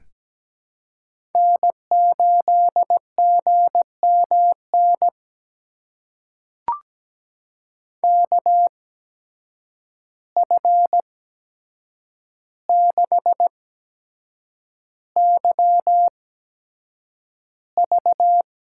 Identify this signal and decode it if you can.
Morse code, heard as a keyed tone, about 17 wpm, spaced as for 7 wpm, 700 Hz. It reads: N8GMN KF6YV
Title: USA Call Signs - 5 Characters - 8x Character Spacing - 17wpm